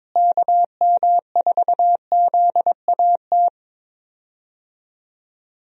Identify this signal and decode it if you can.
Morse code, heard as a keyed tone, about 22 wpm, 700 Hz. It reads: KM4ZAT
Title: USA Call Signs - 6 Characters - Repeated 3x - 22wpm